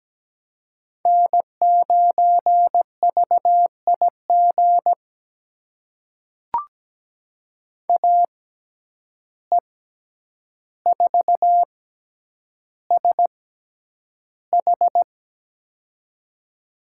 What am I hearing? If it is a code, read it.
Morse code, heard as a keyed tone, about 17 wpm, spaced as for 9 wpm, 700 Hz. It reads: N9VIG AE4SH
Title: USA Call Signs - 5 Characters - 6x Character Spacing - 17wpm